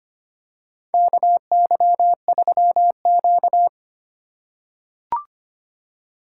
Morse code, heard as a keyed tone, about 25 wpm, 700 Hz. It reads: KY3Q
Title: USA Call Signs - 4 Characters - 4x Character Spacing - 25wpm